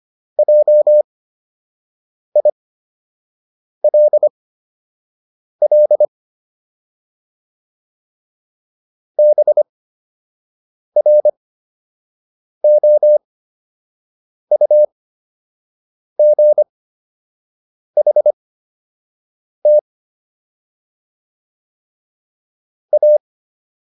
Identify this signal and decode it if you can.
Morse code, heard as a keyed tone, about 25 wpm, spaced as for 6 wpm, 600 Hz. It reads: JILL BROUGHT A